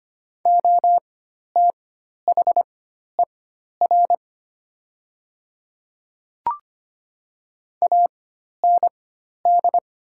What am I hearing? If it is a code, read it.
Morse code, heard as a keyed tone, about 25 wpm, spaced as for 12 wpm, 700 Hz. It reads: OTHER AND
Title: 4x Character Spacing - Top 100 Words - 25wpm